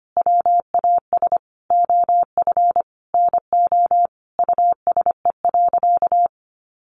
Morse code, heard as a keyed tone, about 25 wpm, 700 Hz. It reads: WAS OF NO USE.
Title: Sentences from Top 100 Words 25wpm